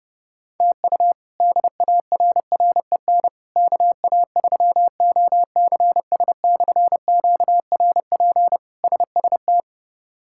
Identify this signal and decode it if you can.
Morse code, heard as a keyed tone, about 32 wpm, 700 Hz. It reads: TU DARREN KA3OCS/QRP SST
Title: K1USN SST QSO Practice - 30wpm